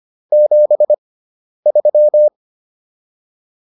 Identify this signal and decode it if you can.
Morse code, heard as a keyed tone, about 25 wpm, spaced as for 10 wpm, 600 Hz. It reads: 73